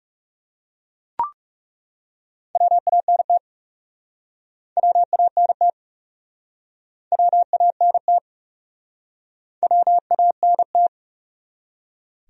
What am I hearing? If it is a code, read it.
Morse code, heard as a keyed tone, about 40 wpm, 700 Hz. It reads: WANT WANT WANT WANT